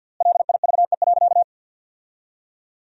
Morse code, heard as a keyed tone, about 50 wpm, 700 Hz. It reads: LIVE.
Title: 3x Word-Spacing - Sentences from Top 300 Words - 50wpm